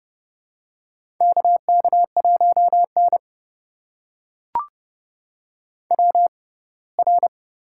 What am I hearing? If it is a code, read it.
Morse code, heard as a keyed tone, about 31 wpm, spaced as for 20 wpm, 700 Hz. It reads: KK1N WR
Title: USA Call Signs - 4 Characters - 6x Character Spacing - 30wpm